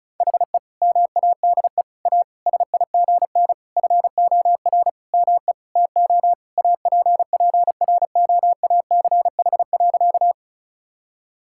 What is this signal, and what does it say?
Morse code, heard as a keyed tone, about 34 wpm, 700 Hz. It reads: HE MADE A SIGN FOR ME TO APPROACH.
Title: Sentences from Top 1000 Words - Review Part 6 - 35wpm